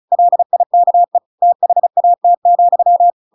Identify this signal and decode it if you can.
Morse code, heard as a keyed tone, about 34 wpm, 700 Hz. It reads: LIKE THAT,